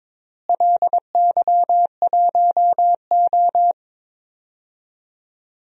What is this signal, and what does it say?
Morse code, heard as a keyed tone, about 22 wpm, 700 Hz. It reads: LY1O
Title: DXCC-100 Call Signs - 4 Characters - Repeated 3x - 22wpm